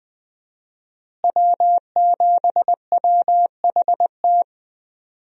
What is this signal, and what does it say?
Morse code, heard as a keyed tone, about 20 wpm, 700 Hz. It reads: W7WHT
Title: USA Call Signs - 5 Characters - Repeated 3x - 20wpm